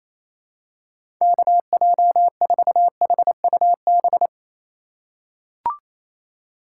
Morse code, heard as a keyed tone, about 27 wpm, 700 Hz. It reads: KJ4HUB
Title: USA Call Signs - 6 Characters - 3x Character Spacing - 28wpm